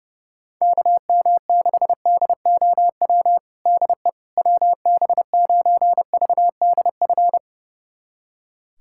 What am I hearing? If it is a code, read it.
Morse code, heard as a keyed tone, about 30 wpm, 700 Hz. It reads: KM6DOW DE WB9VDF
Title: USA Callsign DE Callsign - 6 Characters - 30wpm